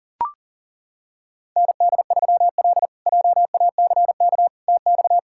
Morse code, heard as a keyed tone, about 40 wpm, 700 Hz. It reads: ND3L JACK TX
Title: K1USN SST QSO Practice - 40wpm